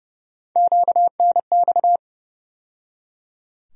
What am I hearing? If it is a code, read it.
Morse code, heard as a keyed tone, about 31 wpm, 700 Hz. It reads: QNX
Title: USA Call Sign Postfixes - 3 Letters 30wpm